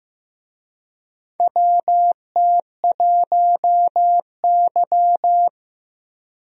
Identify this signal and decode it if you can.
Morse code, heard as a keyed tone, about 15 wpm, 700 Hz. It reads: WT1Y
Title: USA Call Signs - 4 Characters - 15wpm